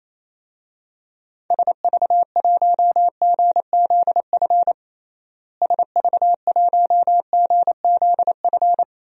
Morse code, heard as a keyed tone, about 28 wpm, 700 Hz. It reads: SV1GZF SV1GZF
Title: DXCC-100 Call Signs - 6 Characters - Repeated 3x - 28wpm